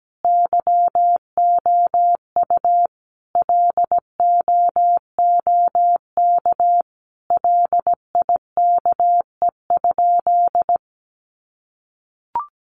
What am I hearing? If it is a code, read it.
Morse code, heard as a keyed tone, about 17 wpm, 700 Hz. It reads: YOU LOOK LIKE?
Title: Sentences from Top 100 Words 17wpm